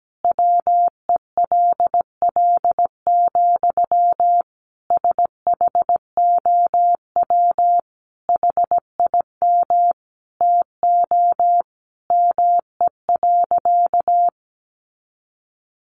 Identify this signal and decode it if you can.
Morse code, heard as a keyed tone, about 17 wpm, 700 Hz. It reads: WELL, SHOW HIM TO ME.
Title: Sentences from Top 200 Words 17wpm